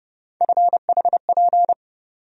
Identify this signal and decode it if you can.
Morse code, heard as a keyed tone, about 30 wpm, 700 Hz. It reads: FHP
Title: USA Call Sign Postfixes - 3 Letters 30wpm